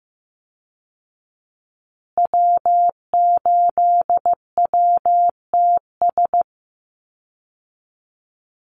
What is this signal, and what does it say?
Morse code, heard as a keyed tone, about 15 wpm, 700 Hz. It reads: W8WTS
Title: Call Signs Repeated 2x - 15wpm